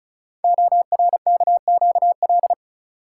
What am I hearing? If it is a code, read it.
Morse code, heard as a keyed tone, about 36 wpm, 700 Hz. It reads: ORKQL
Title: Code Groups - 5 Letters - 35wpm